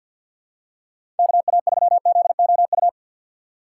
Morse code, heard as a keyed tone, about 47 wpm, 700 Hz. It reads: KA3ZOU
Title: USA Call Signs - 6 Characters - 4x Character Spacing - 50wpm